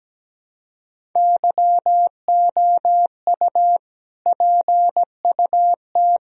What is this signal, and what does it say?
Morse code, heard as a keyed tone, about 17 wpm, 700 Hz. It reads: YOU PUT
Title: Sentences from Top 1000 Words - Review Part 1 - 17wpm